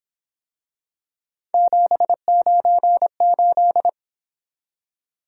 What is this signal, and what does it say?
Morse code, heard as a keyed tone, about 28 wpm, 700 Hz. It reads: 798